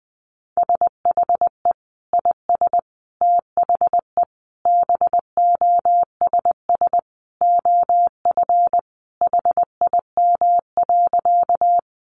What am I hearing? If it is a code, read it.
Morse code, heard as a keyed tone, about 20 wpm, 700 Hz. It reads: SHE IS THE BOSS OF HIM.
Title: Sentences from Top 700 Words 20wpm